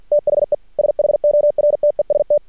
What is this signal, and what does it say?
Morse code, heard as a keyed tone, about 49 wpm, 600 Hz. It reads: THE SHORTEST
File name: archived / CRSnet